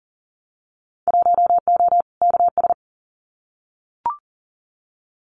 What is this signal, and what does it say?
Morse code, heard as a keyed tone, about 39 wpm, 700 Hz. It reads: 1O KS
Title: Winter Field Day Exchanges 40wpm